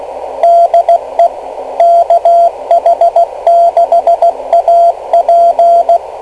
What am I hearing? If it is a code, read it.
Morse code, heard as a keyed tone, about 16 wpm, 680 Hz, with radio noise. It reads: DE KH6AP
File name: kl7fdq / beacons / kh6ap-rst539